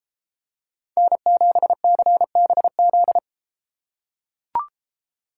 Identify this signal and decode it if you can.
Morse code, heard as a keyed tone, about 32 wpm, 700 Hz. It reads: N7CBZ